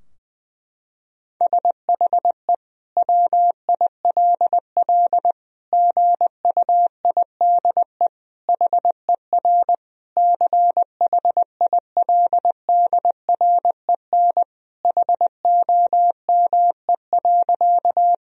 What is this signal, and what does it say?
Morse code, heard as a keyed tone, about 21 wpm, 700 Hz. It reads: SHE WILL GUIDE HER CHILDREN HOME.
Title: Sentences from Top 1000 Words - Review Part 3 - 20wpm